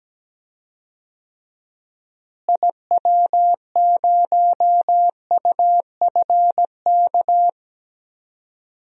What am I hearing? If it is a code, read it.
Morse code, heard as a keyed tone, about 17 wpm, 700 Hz. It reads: IW0UFK
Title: DXCC-100 Call Signs - 6 Characters - Repeated 2x - 17wpm